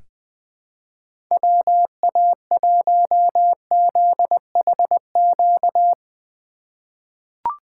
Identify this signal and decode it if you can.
Morse code, heard as a keyed tone, about 20 wpm, 700 Hz. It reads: WA1ZHQ
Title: USA Call Signs - 6 Characters - 4x Character Spacing - 20wpm